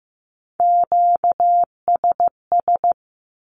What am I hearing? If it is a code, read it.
Morse code, heard as a keyed tone, about 15 wpm, 700 Hz. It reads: QSS
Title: Q-Codes with Spoken Meaning - 15wpm